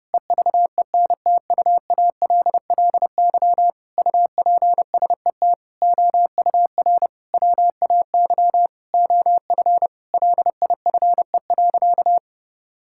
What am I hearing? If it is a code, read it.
Morse code, heard as a keyed tone, about 32 wpm, 700 Hz. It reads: EVENTUALLY UPSET OUR WAY OF LIFE.
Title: Sentences from Top 1000 Words - Review Part 6 - 30wpm